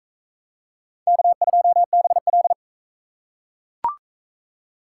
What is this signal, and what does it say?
Morse code, heard as a keyed tone, about 43 wpm, 700 Hz. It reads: K2BL